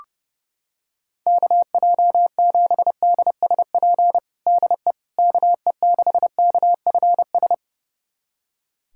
Morse code, heard as a keyed tone, about 32 wpm, 700 Hz. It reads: KJ7DSP DE KE6KFS
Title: USA Callsign DE Callsign - 6 Characters - 30wpm